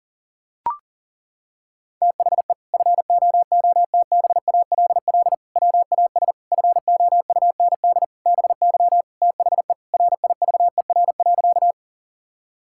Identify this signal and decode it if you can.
Morse code, heard as a keyed tone, about 40 wpm, 700 Hz. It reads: THE FOOTBALL WAS FOUND BY THE RIVER.
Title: Sentences from Top 1600 Words 40wpm